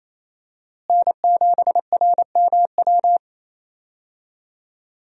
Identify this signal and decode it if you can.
Morse code, heard as a keyed tone, about 28 wpm, 700 Hz. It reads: N7RMW
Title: USA Call Signs - 5 Characters - Repeated 3x - 28wpm